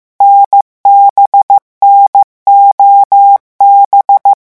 Morse code, heard as a keyed tone, about 15 wpm, 800 Hz. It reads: NBNOB
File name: morse mp3 / revision11 lettres vitesse 14